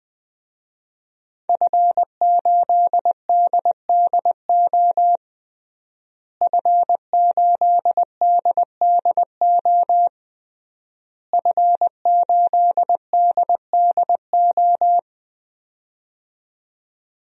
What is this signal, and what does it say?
Morse code, heard as a keyed tone, about 20 wpm, 700 Hz. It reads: F8DDO F8DDO F8DDO
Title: DXCC-100 Call Signs - 5 Characters - Repeated 3x - 20wpm